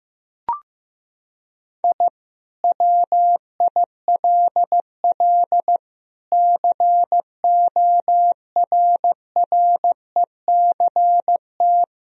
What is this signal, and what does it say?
Morse code, heard as a keyed tone, about 15 wpm, 700 Hz. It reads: I WILL CORRECT
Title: Sentences from Top 1000 Words - Review Part 2 - 15wpm